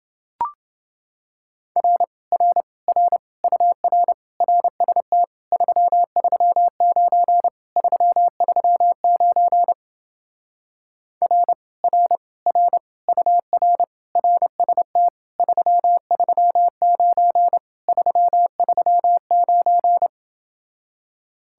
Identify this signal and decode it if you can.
Morse code, heard as a keyed tone, about 28 wpm, 700 Hz. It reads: R R R UR RST 339 339 R R R UR RST 339 339